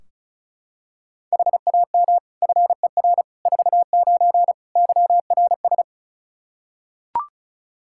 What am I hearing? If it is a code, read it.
Morse code, heard as a keyed tone, about 36 wpm, 700 Hz. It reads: HAM FER 49 YRS
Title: Practice QSO - Exchange 3 - More Info - 35wpm